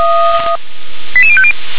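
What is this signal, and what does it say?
Five-tone selective call (ZVEI), heard as quick runs of five tones, tones of about 70 ms. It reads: 88050
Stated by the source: ZVEI five-tone selective call received by SDR.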